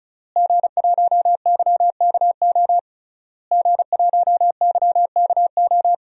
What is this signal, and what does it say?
Morse code, heard as a keyed tone, about 36 wpm, 700 Hz. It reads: G1YKO G1YKO
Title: DXCC-100 Call Signs - 5 Characters - Repeated 3x - 35wpm